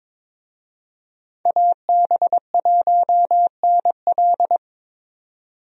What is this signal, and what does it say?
Morse code, heard as a keyed tone, about 21 wpm, 700 Hz. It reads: AB1NL
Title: USA Call Signs - 5 Characters - 22wpm